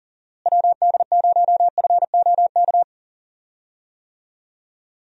Morse code, heard as a keyed tone, about 38 wpm, 700 Hz. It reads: WD0FOK
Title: USA Call Signs - 6 Characters - Repeated 3x - 40wpm